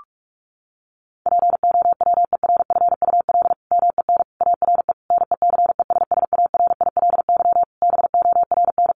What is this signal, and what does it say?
Morse code, heard as a keyed tone, about 44 wpm, 700 Hz. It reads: POWERFUL MEN ARE NECESSARILY BORN
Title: Sentences from Top 1400 Words 45wpm